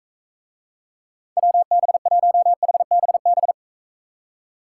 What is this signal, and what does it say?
Morse code, heard as a keyed tone, about 42 wpm, 700 Hz. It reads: WB1HBB